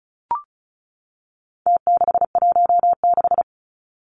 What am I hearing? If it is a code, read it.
Morse code, heard as a keyed tone, about 34 wpm, 700 Hz. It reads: T616
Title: Numbers - Contest Style - 4-Digit 35wpm